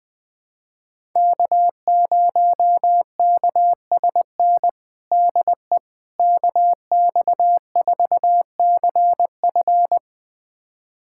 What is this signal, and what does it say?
Morse code, heard as a keyed tone, about 21 wpm, 700 Hz. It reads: K0KSN DE KX4CF
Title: USA Callsign DE Callsign 20wpm